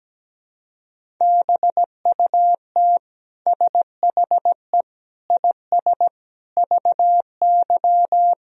Morse code, heard as a keyed tone, about 17 wpm, 700 Hz. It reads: BUT SHE IS VY
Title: Sentences from Top 400 Words and Abbreviations 17wpm